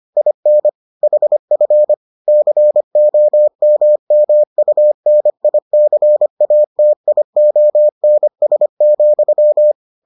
Morse code, heard as a keyed tone, about 25 wpm, 600 Hz. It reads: IN HF COMMUNICATIONS,